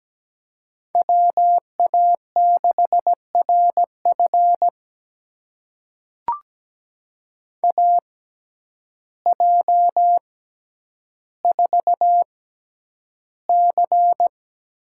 Morse code, heard as a keyed tone, about 17 wpm, spaced as for 9 wpm, 700 Hz. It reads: WA6RF AJ4C
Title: USA Call Signs - 5 Characters - 6x Character Spacing - 17wpm